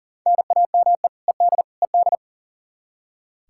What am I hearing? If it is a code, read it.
Morse code, heard as a keyed tone, about 40 wpm, 700 Hz. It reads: NAME ED ED
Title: Practice QSO - Exchange 1 - Beginning - 40wpm